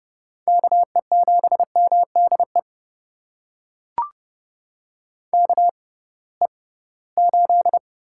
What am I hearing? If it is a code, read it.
Morse code, heard as a keyed tone, about 30 wpm, spaced as for 19 wpm, 700 Hz. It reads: KE7MDE KE8
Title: USA Call Signs - 6 Characters - 6x Character Spacing - 30wpm